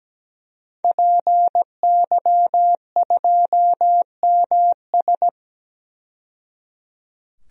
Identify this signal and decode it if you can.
Morse code, heard as a keyed tone, about 17 wpm, 700 Hz. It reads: PY2MS
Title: DXCC-100 Call Signs - 5 Characters - 17wpm